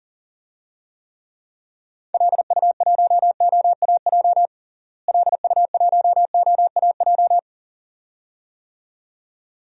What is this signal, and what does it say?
Morse code, heard as a keyed tone, about 40 wpm, 700 Hz. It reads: LU1OAJ LU1OAJ
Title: DXCC-100 Call Signs - 6 Characters - Repeated 2x - 40wpm